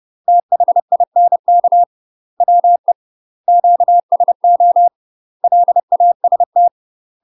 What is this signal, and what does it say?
Morse code, heard as a keyed tone, about 30 wpm, 700 Hz. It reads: THINK WE QSO LAST
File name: UK MASTER QSOs TRAINING / 30WPM / UK MASTER QSOs 0028